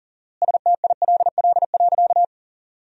Morse code, heard as a keyed tone, about 40 wpm, 700 Hz. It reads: STILL.
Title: 4x Word-Spacing - Sentences from Top 400 Words - 40wpm